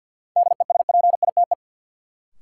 Morse code, heard as a keyed tone, about 53 wpm, 700 Hz. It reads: DESPITE
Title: Sentences from Top 1000 Words - Review Part 7 - New Word Drill - 50wpm